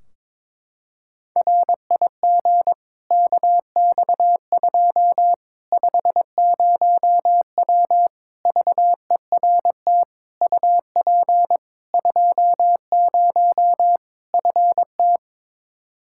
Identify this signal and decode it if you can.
Morse code, heard as a keyed tone, about 22 wpm, 700 Hz. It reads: RIG KX2 50W VERT UP 20 FT
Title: Practice QSO - Exchange 2 - Info - 22wpm